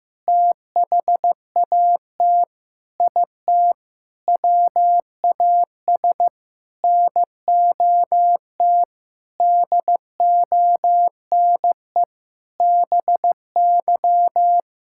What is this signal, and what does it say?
Morse code, heard as a keyed tone, about 15 wpm, 700 Hz. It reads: THAT IT WAS NOT DONE BY
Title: Sentences from Top 1000 Words - Review Part 5 - 15wpm